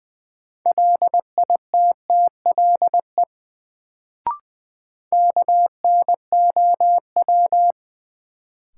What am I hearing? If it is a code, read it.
Morse code, heard as a keyed tone, about 20 wpm, 700 Hz. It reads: LITTLE KNOW
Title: Sets of 2 Words - New Word Drill - 20wpm